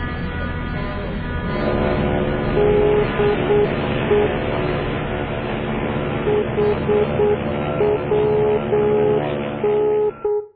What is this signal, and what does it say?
Morse code, heard as a keyed tone, about 8 wpm, 420 Hz, with noise. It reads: DE HWN